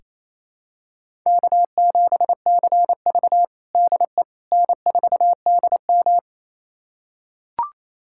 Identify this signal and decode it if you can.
Morse code, heard as a keyed tone, about 28 wpm, 700 Hz. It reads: K7CV DE N4DM
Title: USA Callsign DE Callsign - 4 Characters - 28wpm